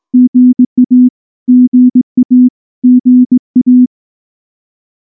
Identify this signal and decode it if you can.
Morse code, heard as a keyed tone, about 19 wpm, 250 Hz. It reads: GA GA GA
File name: morse / morse-training-20WPM-8